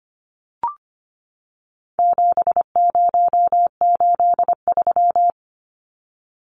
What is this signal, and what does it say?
Morse code, heard as a keyed tone, about 25 wpm, 700 Hz. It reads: 7083
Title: Numbers - 4-Digit 25wpm